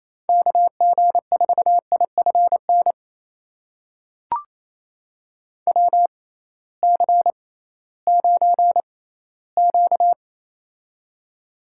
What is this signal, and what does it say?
Morse code, heard as a keyed tone, about 28 wpm, spaced as for 16 wpm, 700 Hz. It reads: KG4IFN WC9Q
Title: USA Call Signs - 6x Character Spacing - 28wpm